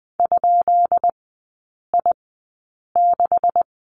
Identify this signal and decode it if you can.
Morse code, heard as a keyed tone, about 20 wpm, spaced as for 8 wpm, 700 Hz. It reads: ?I6
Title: 133 Lesson 22e - Question Mark - Three Characters 20wpm